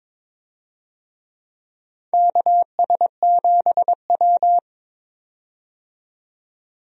Morse code, heard as a keyed tone, about 22 wpm, 700 Hz. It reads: KS7W
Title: USA Call Signs - 4 Characters - 22wpm